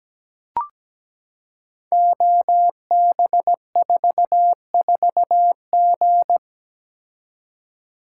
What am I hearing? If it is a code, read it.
Morse code, heard as a keyed tone, about 17 wpm, 700 Hz. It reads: OB44G
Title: Code Groups - 5 Letter:Numbers - 17wpm